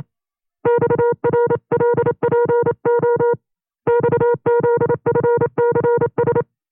Morse code, heard as a keyed tone, about 28 wpm, 480 Hz. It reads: XRLPO XZFCS
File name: 25WPM